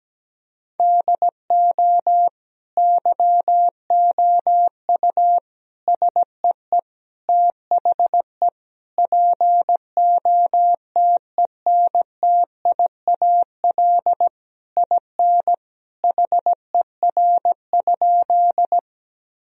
Morse code, heard as a keyed tone, about 17 wpm, 700 Hz. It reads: DO YOU SEE THE POTENTIAL IN HER?
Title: Sentences from Top 1000 Words - Review Part 4 - 17wpm